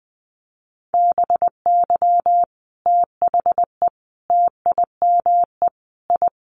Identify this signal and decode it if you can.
Morse code, heard as a keyed tone, about 20 wpm, 700 Hz. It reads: BY THE TIME I
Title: Sets of 5 Words Encore2 20wpm